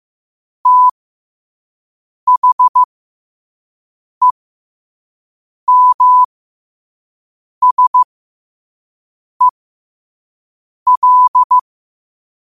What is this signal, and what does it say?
Morse code, heard as a keyed tone, about 15 wpm, spaced as for 5 wpm, 1000 Hz. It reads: THEMSEL
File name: qotdfiles / 5wpm